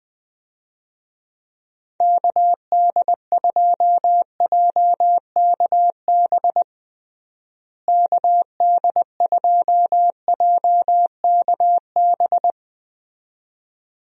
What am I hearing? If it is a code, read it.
Morse code, heard as a keyed tone, about 20 wpm, 700 Hz. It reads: KD2JKB KD2JKB
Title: USA Call Signs - 6 Characters - Repeated 2x - 20wpm